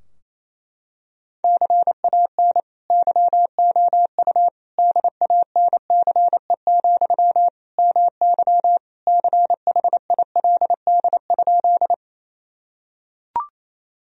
Morse code, heard as a keyed tone, about 28 wpm, 700 Hz. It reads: CAN YOU DANCE, MY CHILD?
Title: Sentences from Top 1000 Words - Review Part 4 - 28wpm